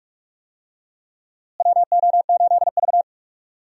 Morse code, heard as a keyed tone, about 41 wpm, 700 Hz. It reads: WO8V
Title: USA Call Signs - 4 Characters - Repeated 2x - 45wpm